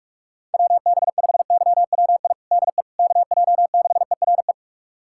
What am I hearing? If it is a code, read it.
Morse code, heard as a keyed tone, about 46 wpm, 700 Hz. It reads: WB5YWI DE KJ6ERE
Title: USA Callsign DE Callsign - 6 Characters - 45wpm